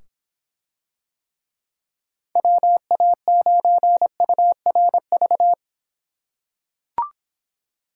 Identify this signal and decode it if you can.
Morse code, heard as a keyed tone, about 26 wpm, 700 Hz. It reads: WA9URV